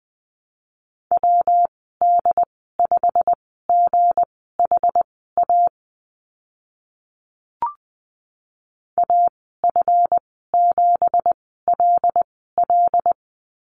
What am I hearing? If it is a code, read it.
Morse code, heard as a keyed tone, about 20 wpm, spaced as for 15 wpm, 700 Hz. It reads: WD5GHA AF7LL
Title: 186 Lesson 30g - Letter b - Call Signs 20wpm